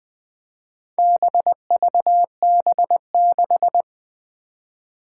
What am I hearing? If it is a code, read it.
Morse code, heard as a keyed tone, about 20 wpm, 700 Hz. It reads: BVB6
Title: Four Challenging Letter-Number Combinations 20wpm